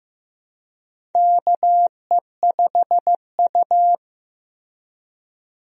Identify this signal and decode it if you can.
Morse code, heard as a keyed tone, about 15 wpm, 700 Hz. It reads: KE5U
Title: USA Call Signs - 4 Characters - 15wpm